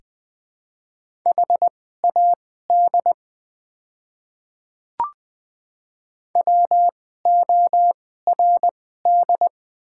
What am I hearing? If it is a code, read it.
Morse code, heard as a keyed tone, about 21 wpm, spaced as for 15 wpm, 690 Hz. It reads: HAD WORD
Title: 2x Character Spacing - Top 100 Words - 20wpm